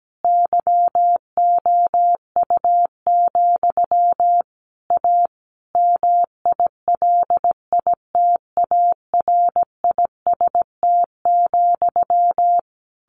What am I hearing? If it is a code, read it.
Morse code, heard as a keyed tone, about 17 wpm, 700 Hz. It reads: YOU, A MILITARIST,
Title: A Sound Of Thunder 17wpm